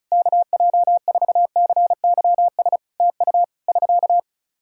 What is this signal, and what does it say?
Morse code, heard as a keyed tone, about 34 wpm, 700 Hz. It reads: KJ4CYS TU <SK>
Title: K1USN SST QSO Practice 35wpm